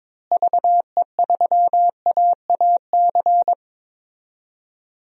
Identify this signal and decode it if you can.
Morse code, heard as a keyed tone, about 22 wpm, 700 Hz. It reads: VE3AAC
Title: DXCC-100 Call Signs - 6 Characters - Repeated 3x - 22wpm